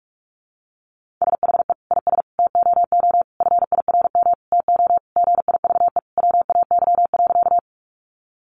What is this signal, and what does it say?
Morse code, heard as a keyed tone, about 44 wpm, 700 Hz. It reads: SHE IS TOO FIRM TO GIVE WAY.
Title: Sentences from Top 1400 Words 45wpm